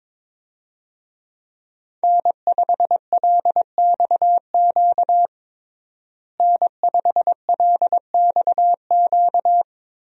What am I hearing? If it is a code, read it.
Morse code, heard as a keyed tone, about 22 wpm, 700 Hz. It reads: N5LXQ N5LXQ
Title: USA Call Signs - 5 Characters - Repeated 2x - 22wpm